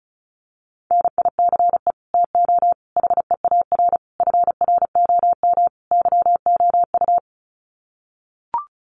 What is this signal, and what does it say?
Morse code, heard as a keyed tone, about 34 wpm, 700 Hz. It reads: NICE TO HEAR FROM YOU
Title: Common QSO Phrases 35wpm